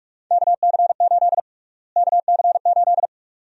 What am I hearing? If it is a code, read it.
Morse code, heard as a keyed tone, about 46 wpm, 700 Hz. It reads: KC8 KC8
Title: USA Call Sign Prefixes - Repeated 2x - 45wpm